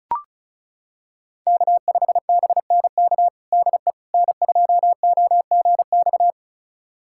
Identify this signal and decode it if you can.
Morse code, heard as a keyed tone, about 36 wpm, 700 Hz. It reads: K5BNK DE N2OGX
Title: USA Callsign DE Callsign - 5 Characters - 35wpm